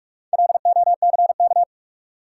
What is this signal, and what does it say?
Morse code, heard as a keyed tone, about 46 wpm, 700 Hz. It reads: LOCK
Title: Sentences from Top 1300 Words - New Word Drill - 45wpm